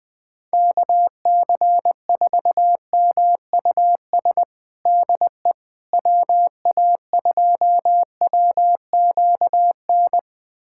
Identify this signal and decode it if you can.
Morse code, heard as a keyed tone, about 20 wpm, 700 Hz. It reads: KC4MUS DE WA2WQN
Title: USA Callsign DE Callsign 20wpm